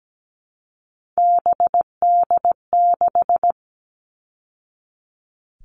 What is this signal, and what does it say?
Morse code, heard as a keyed tone, about 17 wpm, 700 Hz. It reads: BD6
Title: All Cut Numbers - 3-Digit 17wpm